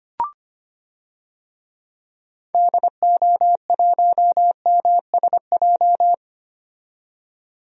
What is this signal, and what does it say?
Morse code, heard as a keyed tone, about 25 wpm, 700 Hz. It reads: DO1MSJ
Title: DXCC-100 Call Signs - 6 Characters - 25wpm